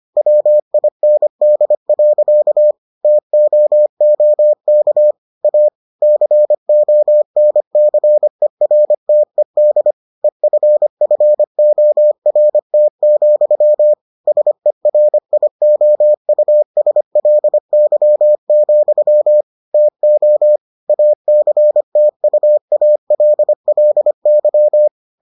Morse code, heard as a keyed tone, about 25 wpm, 600 Hz. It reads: WIND. TOOK A CONCERTED EFFORT, SERIOUSLY, TO ACTUALLY